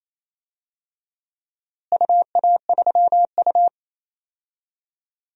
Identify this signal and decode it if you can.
Morse code, heard as a keyed tone, about 27 wpm, 700 Hz. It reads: UA3U
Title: DXCC-100 Call Signs - 4 Characters - Repeated 2x - 28wpm